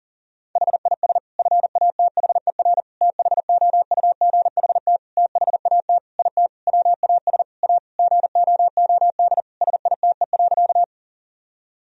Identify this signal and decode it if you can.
Morse code, heard as a keyed tone, about 37 wpm, 700 Hz. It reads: HIS FATHER THOUGHT THAT IT WAS A GOOD SITE.
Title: Sentences from Top 1000 Words - Review Part 4 - 40wpm